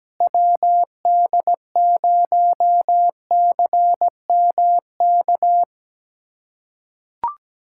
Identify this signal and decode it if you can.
Morse code, heard as a keyed tone, about 17 wpm, 700 Hz. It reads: WD0CMK
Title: USA Callsign DE Callsign - 6 Characters - 17wpm